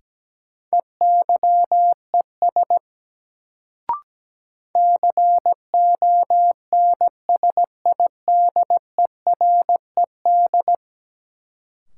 Morse code, heard as a keyed tone, about 17 wpm, 700 Hz. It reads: EYES CONSIDERED
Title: Sets of 5 Words - New Word Drill - 17wpm